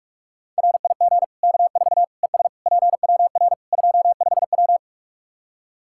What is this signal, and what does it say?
Morse code, heard as a keyed tone, about 41 wpm, 700 Hz. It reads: RIG K4 ES PWR 25W
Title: Practice QSO - Exchange 2 - Info - 45wpm